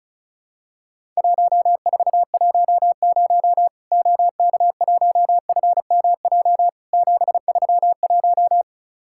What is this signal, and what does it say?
Morse code, heard as a keyed tone, about 36 wpm, 700 Hz. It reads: 1410 OK1FMJ 731
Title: Worked All Europe DX Contest - QTC Exchange - 35wpm